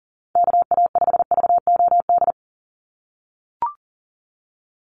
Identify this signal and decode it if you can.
Morse code, heard as a keyed tone, about 38 wpm, 700 Hz. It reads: KA5VOD
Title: USA Call Signs 40wpm